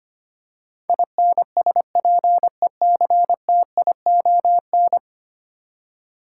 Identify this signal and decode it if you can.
Morse code, heard as a keyed tone, about 25 wpm, 700 Hz. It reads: INSPECTION
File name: 10 Letter Words - First 3 Letters Given - 25wpm